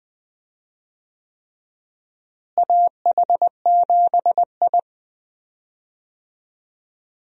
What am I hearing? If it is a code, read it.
Morse code, heard as a keyed tone, about 20 wpm, 700 Hz. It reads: AH7I
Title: USA Call Signs - 4 Characters - Repeated 3x - 20wpm